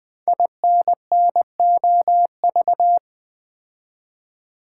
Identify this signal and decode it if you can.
Morse code, heard as a keyed tone, about 20 wpm, 700 Hz. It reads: INNOV
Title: Build A Word - 5 Letters At A Time - 10-Letter Words - 20wpm